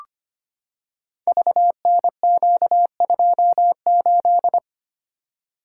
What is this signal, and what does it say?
Morse code, heard as a keyed tone, about 25 wpm, 700 Hz. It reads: VNQ28
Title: Code Groups - 5 Letter:Numbers - 25wpm